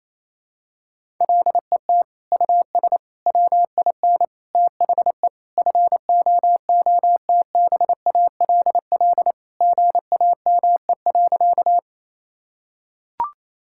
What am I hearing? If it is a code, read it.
Morse code, heard as a keyed tone, about 28 wpm, 700 Hz. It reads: LET US WIN THE FOOTBALL GAME.
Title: Sentences from Top 1600 Words 28wpm